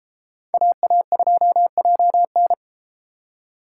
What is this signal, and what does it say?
Morse code, heard as a keyed tone, about 32 wpm, 700 Hz. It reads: AA2JN